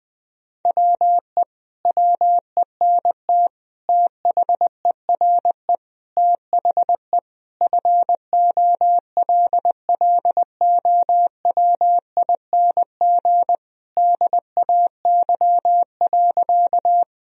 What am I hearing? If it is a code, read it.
Morse code, heard as a keyed tone, about 21 wpm, 700 Hz. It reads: WE WENT THERE THE FOLLOWING DAY.
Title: Sentences from Top 1000 Words - Review Part 4 - 20wpm